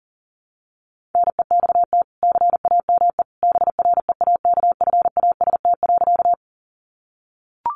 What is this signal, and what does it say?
Morse code, heard as a keyed tone, about 40 wpm, 700 Hz. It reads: NEXT CAME BREAKFAST.
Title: Sentences from Top 1500 Words 40wpm